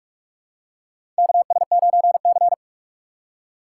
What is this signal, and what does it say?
Morse code, heard as a keyed tone, about 47 wpm, 700 Hz. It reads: KS9C
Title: USA Call Signs - 3x Character Spacing - 45wpm